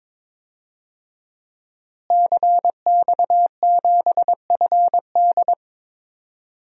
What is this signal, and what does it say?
Morse code, heard as a keyed tone, about 23 wpm, 700 Hz. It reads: CX7FD
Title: DXCC-100 Call Signs - 5 Characters - Repeated 2x - 22wpm